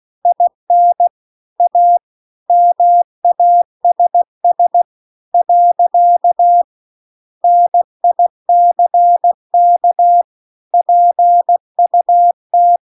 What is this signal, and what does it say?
Morse code, heard as a keyed tone, about 16 wpm, 700 Hz. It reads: IN A MASS . NICK PUT
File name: CW-The-Big-Twoheaded-River-2-16WPM